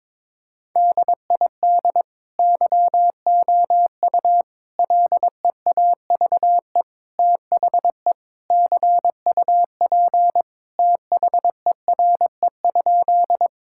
Morse code, heard as a keyed tone, about 22 wpm, 700 Hz. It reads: DID YOU LEAVE THE CUP THERE?
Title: Sentences from Top 1600 Words 22wpm